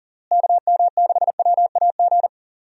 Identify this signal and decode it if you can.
Morse code, heard as a keyed tone, about 37 wpm, 700 Hz. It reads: KM6WAG
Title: USA Call Signs - 6 Characters - Repeated 3x - 40wpm